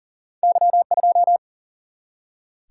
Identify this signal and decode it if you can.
Morse code, heard as a keyed tone, about 38 wpm, 700 Hz. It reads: Y2
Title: Single-Letter Single-Number 40wpm